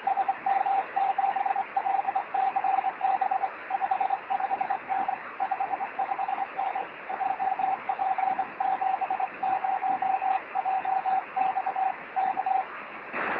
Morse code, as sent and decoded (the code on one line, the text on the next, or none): none